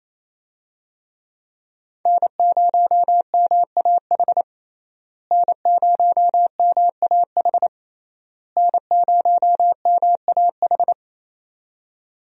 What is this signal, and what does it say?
Morse code, heard as a keyed tone, about 28 wpm, 700 Hz. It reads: N0MAH N0MAH N0MAH
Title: USA Call Signs - 5 Characters - Repeated 3x - 28wpm